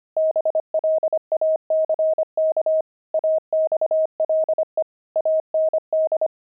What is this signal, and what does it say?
Morse code, heard as a keyed tone, about 25 wpm, 630 Hz. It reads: BLACK AXLE AND
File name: pangram-fs-25-cs-25-axle